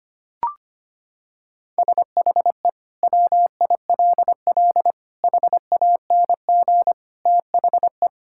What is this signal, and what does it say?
Morse code, heard as a keyed tone, about 25 wpm, 700 Hz. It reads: SHE WILL HANG THE
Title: Sentences from Top 1600 Words 25wpm